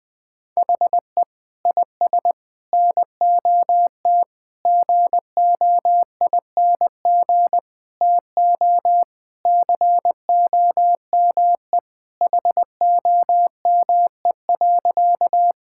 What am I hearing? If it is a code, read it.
Morse code, heard as a keyed tone, about 20 wpm, 700 Hz. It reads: HE IS NOT GOING TO COME HOME.
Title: Sentences with 4-Word Phrases - Spoken Beginning - 20wpm